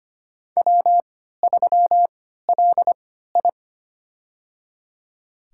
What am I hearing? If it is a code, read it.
Morse code, heard as a keyed tone, about 25 wpm, spaced as for 14 wpm, 700 Hz. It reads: W3LI
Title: USA Call Signs - 3x Character Spacing - 25wpm